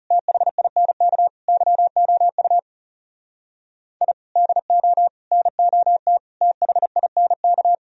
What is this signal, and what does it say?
Morse code, heard as a keyed tone, about 36 wpm, 700 Hz. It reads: THINK YOU I DO NOT THINK
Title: Speed-Racing - Sets of 5 Words 30wpm